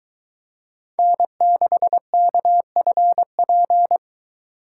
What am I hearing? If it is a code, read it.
Morse code, heard as a keyed tone, about 21 wpm, 700 Hz. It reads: N6KFP